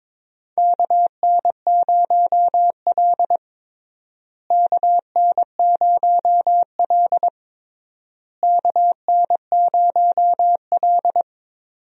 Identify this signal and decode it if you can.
Morse code, heard as a keyed tone, about 21 wpm, 700 Hz. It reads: KN0L KN0L KN0L
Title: USA Call Signs - 4 Characters - Repeated 3x - 22wpm